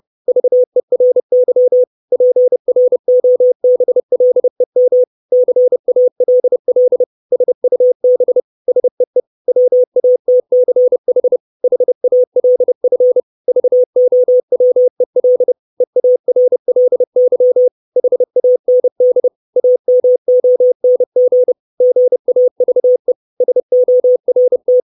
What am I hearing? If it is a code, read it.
Morse code, heard as a keyed tone, about 30 wpm, 500 Hz. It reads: VERY PROBLEM CALL SUB SEE WATCH HALF VOWEL EARLY HAND AMONG GAVE SORT
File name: 632 Words 30WPM